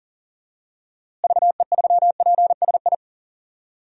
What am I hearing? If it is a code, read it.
Morse code, heard as a keyed tone, about 40 wpm, 700 Hz. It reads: VE3PSI
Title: DXCC-100 Call Signs - 6 Characters - Repeated 2x - 40wpm